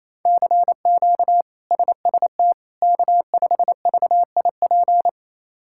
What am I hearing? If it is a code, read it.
Morse code, heard as a keyed tone, about 28 wpm, 700 Hz. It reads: CQ SST K5VIP
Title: K1USN SST QSO Practice - 28wpm